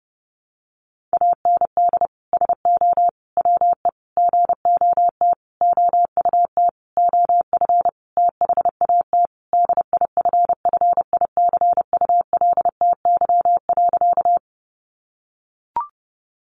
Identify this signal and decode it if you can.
Morse code, heard as a keyed tone, about 30 wpm, 700 Hz. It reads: AND SO WE GOT OUT OF THAT DIFFICULTY.
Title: Sentences from Top 1500 Words 30wpm